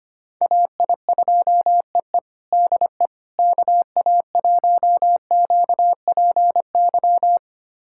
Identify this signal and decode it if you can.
Morse code, heard as a keyed tone, about 25 wpm, 700 Hz. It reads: AI2EE DE KA1QPY
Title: USA Callsign DE Callsign 25wpm